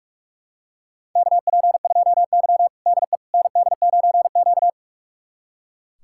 Morse code, heard as a keyed tone, about 44 wpm, 700 Hz. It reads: KP2Y DE ND9Q
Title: USA Callsign DE Callsign - 4 Characters - 45wpm